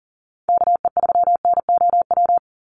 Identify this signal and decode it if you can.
Morse code, heard as a keyed tone, about 40 wpm, 700 Hz. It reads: KE3NOW
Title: USA Call Signs 40wpm